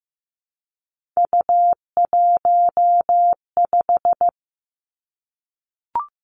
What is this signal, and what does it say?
Morse code, heard as a keyed tone, about 15 wpm, 700 Hz. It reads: U15
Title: Three Challenging Letter-Number Combinations 15wpm